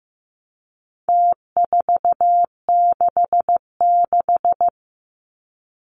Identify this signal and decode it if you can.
Morse code, heard as a keyed tone, about 15 wpm, 700 Hz. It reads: T466
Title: Numbers - Contest Style - 4-Digit 15wpm